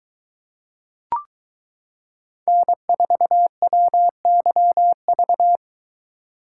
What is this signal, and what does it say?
Morse code, heard as a keyed tone, about 22 wpm, 700 Hz. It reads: N4WYV